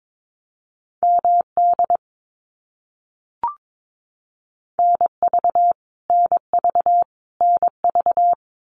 Morse code, heard as a keyed tone, about 22 wpm, 700 Hz. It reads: MD NV NV NV
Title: US States Repeated 3x 22wpm